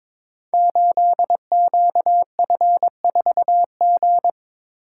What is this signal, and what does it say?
Morse code, heard as a keyed tone, about 22 wpm, 700 Hz. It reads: 8QF4G
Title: Code Groups - 5 Letter:Numbers - 22wpm